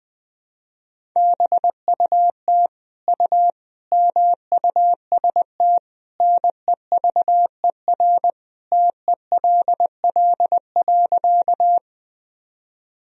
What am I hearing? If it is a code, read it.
Morse code, heard as a keyed tone, about 21 wpm, 700 Hz. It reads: BUT U MUST NEVER TELL.
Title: Sentences from Top 300 Words and Abbreviations 20wpm